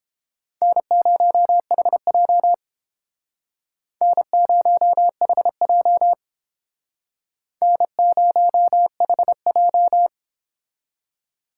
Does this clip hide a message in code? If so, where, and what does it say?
Yes: Morse code from 0.6 to 10.1 s, heard as a keyed tone, about 32 wpm, 700 Hz. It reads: N0HJ N0HJ N0HJ